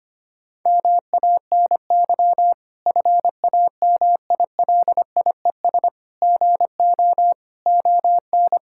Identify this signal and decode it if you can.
Morse code, heard as a keyed tone, about 25 wpm, 700 Hz. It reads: MANY FAMILIES GO ON
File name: practice / 25wpm / Difficult Sentences with Hints - 25wpm